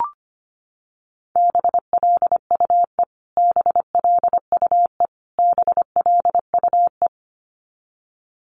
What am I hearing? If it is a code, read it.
Morse code, heard as a keyed tone, about 25 wpm, 700 Hz. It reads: BLUE BLUE BLUE
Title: Top 500 Words Repeated 25wpm